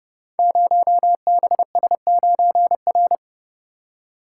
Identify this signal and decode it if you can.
Morse code, heard as a keyed tone, about 30 wpm, 700 Hz. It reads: 0BS9R
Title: Code Groups - 5 Letter:Numbers - 30wpm